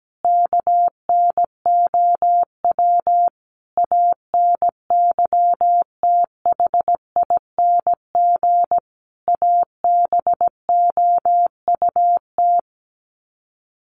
Morse code, heard as a keyed tone, about 17 wpm, 700 Hz. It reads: KNOW ANYTHING ABOUT
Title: Sets of 5 Words 17wpm